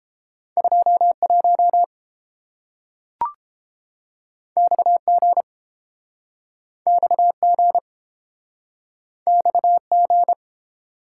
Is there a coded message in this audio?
Yes: Morse code from 0.6 to 10.3 s, heard as a keyed tone, about 30 wpm, 700 Hz. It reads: 21 XG XG XG